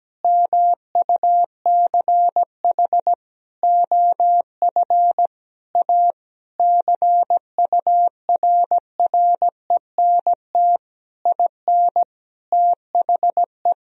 Morse code, heard as a keyed tone, about 17 wpm, 700 Hz. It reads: MUCH OF A CURRENT IN THE
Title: Sentences from Top 1000 Words - Review Part 4 - 17wpm